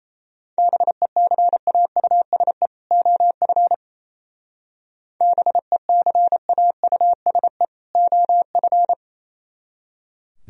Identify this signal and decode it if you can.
Morse code, heard as a keyed tone, about 32 wpm, 690 Hz. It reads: BECAUSE OF BECAUSE OF
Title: Speed-Racing - Sets of 2 Words 28wpm